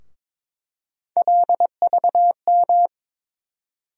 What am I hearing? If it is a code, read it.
Morse code, heard as a keyed tone, about 22 wpm, 700 Hz. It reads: LVM
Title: USA Call Sign Postfixes - 3 Letters 22wpm